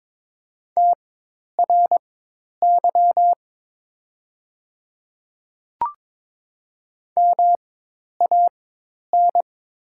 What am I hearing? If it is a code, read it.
Morse code, heard as a keyed tone, about 22 wpm, spaced as for 10 wpm, 700 Hz. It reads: TRY MAN
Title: 4x Character Spacing - Top 200 Words - 22wpm